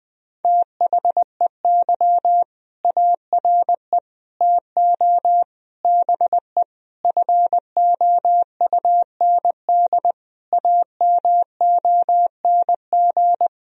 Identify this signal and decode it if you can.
Morse code, heard as a keyed tone, about 20 wpm, 700 Hz. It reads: THEY ARE TO BE FOUND AMONG
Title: Sentences from Top 1000 Words - Review Part 3 - 20wpm